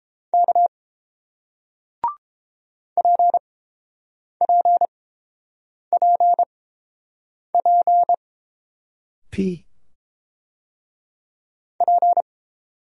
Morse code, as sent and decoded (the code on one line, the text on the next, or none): -.- / .--. .--. .--. .--. / .--.
K PPPP P